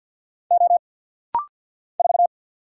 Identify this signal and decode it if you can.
Morse code, heard as a keyed tone, about 50 wpm, 700 Hz. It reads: O 4
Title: Single Letter-Number Alt - Rapid-Fire 50wpm